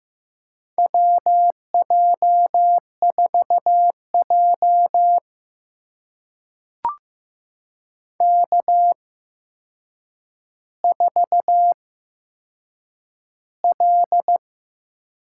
Morse code, heard as a keyed tone, about 15 wpm, spaced as for 7 wpm, 700 Hz. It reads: WJ4J K4L
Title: USA Call Signs - 4 Characters - 8x Character Spacing - 15wpm